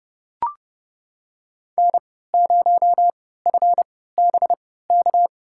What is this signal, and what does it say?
Morse code, heard as a keyed tone, about 30 wpm, spaced as for 17 wpm, 700 Hz. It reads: N0FBK
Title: USA Call Signs - 5 Characters - 3x Character Spacing - 30wpm